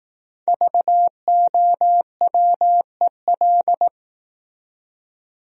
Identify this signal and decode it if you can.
Morse code, heard as a keyed tone, about 19 wpm, 700 Hz. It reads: VOWEL
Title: Speed-Racing Top 500 Words 15wpm